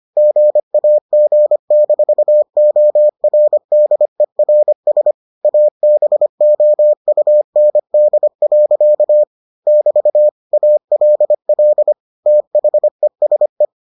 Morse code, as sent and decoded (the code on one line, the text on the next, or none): --. .- --. -....- --- .-. -.. .-. ... / .- -... --- ..- -. -.. .-.-.- / -...- / .- .-.. .-.. / - .... ... .
GAG-ORDERS ABOUND. <BT> ALL THESE